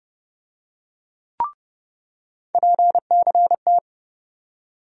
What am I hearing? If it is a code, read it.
Morse code, heard as a keyed tone, about 30 wpm, 700 Hz. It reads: PCT
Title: USA Call Sign Postfixes - 3 Letters 30wpm